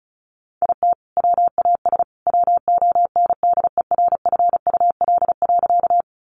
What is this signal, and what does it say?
Morse code, heard as a keyed tone, about 34 wpm, 700 Hz. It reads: IT WAS WONDERFUL.
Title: Sentences from Top 1100 Words 35wpm